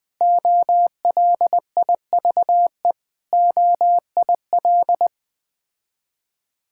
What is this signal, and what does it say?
Morse code, heard as a keyed tone, about 20 wpm, 700 Hz. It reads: OLIVE OIL
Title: Difficult Words with Hints - 20wpm